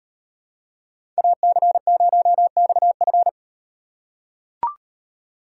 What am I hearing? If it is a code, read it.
Morse code, heard as a keyed tone, about 38 wpm, 700 Hz. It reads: AC0XF